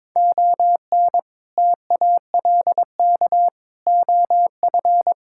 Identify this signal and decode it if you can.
Morse code, heard as a keyed tone, about 23 wpm, 700 Hz. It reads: ON TALK OF
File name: practice / 22wpm / Build A Phrase - 5-Word Phrases - 22wpm